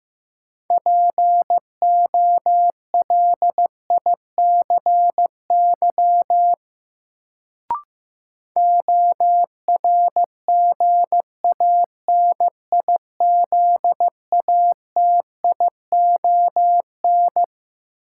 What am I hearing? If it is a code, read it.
Morse code, heard as a keyed tone, about 15 wpm, 700 Hz. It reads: POLICY ORGANIZATION
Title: Sentences from Top 1000 Words - Review Part 5 - New Word Drill - 15wpm